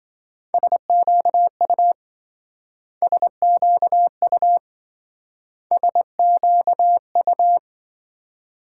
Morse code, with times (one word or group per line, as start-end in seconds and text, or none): 0.5-1.9 SQU
3.0-4.6 SQU
5.7-7.6 SQU